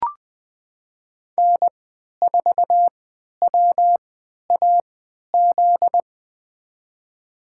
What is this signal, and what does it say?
Morse code, heard as a keyed tone, about 20 wpm, spaced as for 11 wpm, 700 Hz. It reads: N4WAZ
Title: USA Call Signs - 5 Characters - 3x Character Spacing - 20wpm